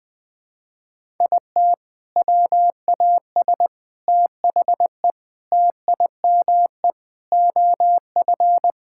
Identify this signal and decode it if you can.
Morse code, heard as a keyed tone, about 20 wpm, 700 Hz. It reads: IT WAS THE TIME OF
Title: Sentences with 4-Word Phrases - 20wpm